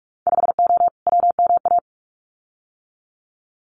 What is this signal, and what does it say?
Morse code, heard as a keyed tone, about 46 wpm, 700 Hz. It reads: 5O WMA
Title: Winter Field Day Exchanges Encore 45wpm